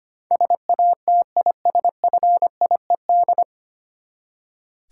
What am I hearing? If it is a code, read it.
Morse code, heard as a keyed tone, about 25 wpm, 700 Hz. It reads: SATISFIED
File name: Nine Letter Words - 25wpm